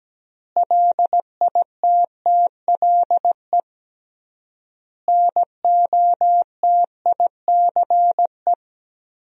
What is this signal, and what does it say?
Morse code, heard as a keyed tone, about 17 wpm, 700 Hz. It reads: LITTLE NOTICE
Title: 3x Word-Spacing - Sentences from Top 400 Words - 17wpm